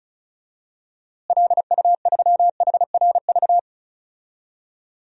Morse code, heard as a keyed tone, about 36 wpm, 700 Hz. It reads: LU3HRV
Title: DXCC-100 Call Signs - 6 Characters - Repeated 3x - 35wpm